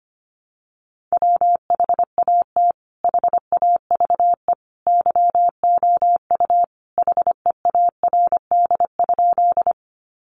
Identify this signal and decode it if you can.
Morse code, heard as a keyed tone, about 25 wpm, 700 Hz. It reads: WHAT HAVE YOU HEARD?
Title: Sentences from Top 400 Words 25wpm